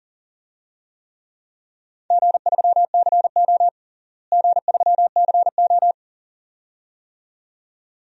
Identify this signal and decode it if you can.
Morse code, heard as a keyed tone, about 40 wpm, 700 Hz. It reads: G3CO G3CO
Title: DXCC-100 Call Signs - 4 Characters - Repeated 2x - 40wpm